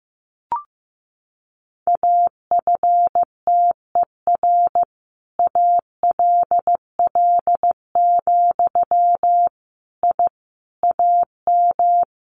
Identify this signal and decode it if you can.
Morse code, heard as a keyed tone, about 15 wpm, 700 Hz. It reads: AFTER ALL, I AM
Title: Sentences from Top 600 Words 15wpm